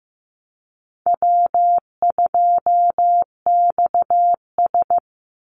Morse code, heard as a keyed tone, about 15 wpm, 700 Hz. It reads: W2XS
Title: Call Signs Repeated 3x - 15wpm